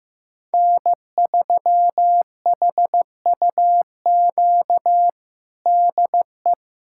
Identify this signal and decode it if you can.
Morse code, heard as a keyed tone, about 15 wpm, 700 Hz. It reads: N3HUQ DE
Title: USA Callsign DE Callsign - 5 Characters - 15wpm